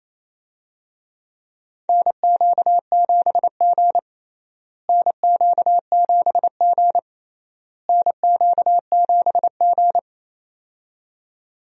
Morse code, heard as a keyed tone, about 27 wpm, 700 Hz. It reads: NQ7G NQ7G NQ7G
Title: USA Call Signs - 4 Characters - Repeated 3x - 28wpm